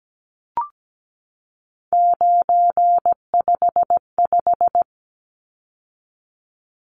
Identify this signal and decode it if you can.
Morse code, heard as a keyed tone, about 17 wpm, 700 Hz. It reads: 955
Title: Numbers - 3-Digit 17wpm